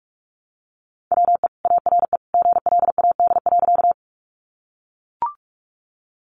Morse code, heard as a keyed tone, about 44 wpm, 700 Hz. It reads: WE ARE GLAD.
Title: Sentences from Top 800 Words 45wpm